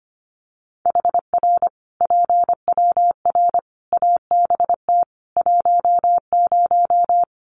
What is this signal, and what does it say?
Morse code, heard as a keyed tone, about 25 wpm, 700 Hz. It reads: HR PWR ABT 10
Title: Common QSO Phrases 25wpm